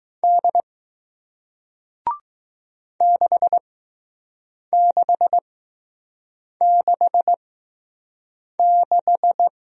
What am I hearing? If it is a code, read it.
Morse code, heard as a keyed tone, about 20 wpm, spaced as for 6 wpm, 700 Hz. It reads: D 6666